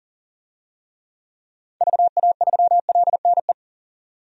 Morse code, heard as a keyed tone, about 37 wpm, 700 Hz. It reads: VA3LNE
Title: DXCC-100 Call Signs - 6 Characters - 40wpm